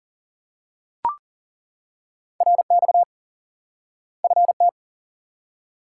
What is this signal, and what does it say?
Morse code, heard as a keyed tone, about 40 wpm, 700 Hz. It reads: RX FT
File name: IQE Course 3 - Lesson 89 - YAGI - Two Elements - 40wpm